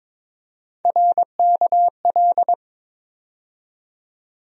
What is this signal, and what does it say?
Morse code, heard as a keyed tone, about 22 wpm, 700 Hz. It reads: RKL
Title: USA Call Sign Postfixes - 3 Letters 22wpm